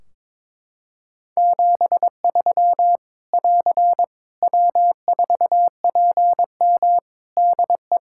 Morse code, heard as a keyed tone, about 22 wpm, 700 Hz. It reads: 73 <AR> W4PM DE
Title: K1USN SST QSO Practice 22wpm